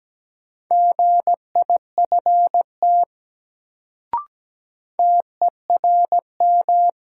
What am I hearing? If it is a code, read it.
Morse code, heard as a keyed tone, about 17 wpm, 700 Hz. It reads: GIFT TERM
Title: Sentences from Top 900 Words - New Word Drill - 17wpm